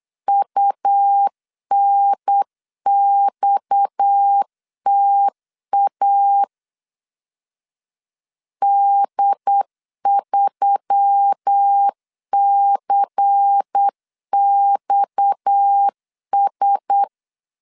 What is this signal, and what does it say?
Morse code, heard as a keyed tone, about 8 wpm, 800 Hz. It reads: UNXTA D3CXS
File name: LEZIONI / 6 PKX0 / 6 PKX 8wpm 3041